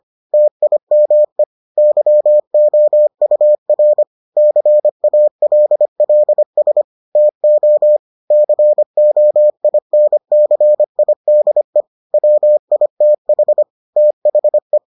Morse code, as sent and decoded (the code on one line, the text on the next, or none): - .. -- . / -.-- --- ..- .-. / -.-. .- .-.. .-.. ... / - --- / -.-. --- .. -. -.-. .. -.. / .-- .. - .... / - .... .
TIME YOUR CALLS TO COINCIDE WITH THE